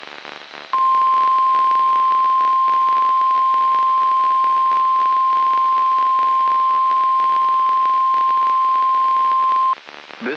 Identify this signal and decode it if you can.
Emergency Alert System (SAME) tones (the 1050 Hz attention tone). It is heard from 0.7 to 9.6 s.
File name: eas messages / EAS WXM95 Towanda